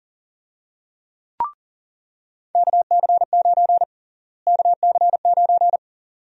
Morse code, heard as a keyed tone, about 41 wpm, 700 Hz. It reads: KC9 KC9
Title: USA Call Sign Prefixes - Repeated 2x - 40wpm